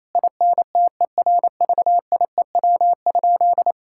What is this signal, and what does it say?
Morse code, heard as a keyed tone, about 28 wpm, 700 Hz. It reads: INTERVIEW?
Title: Sentences from Top 1600 Words 28wpm